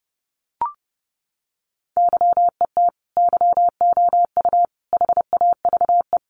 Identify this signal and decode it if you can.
Morse code, heard as a keyed tone, about 30 wpm, 700 Hz. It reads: YET YOU HAVE
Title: Sentences from Top 600 Words 30wpm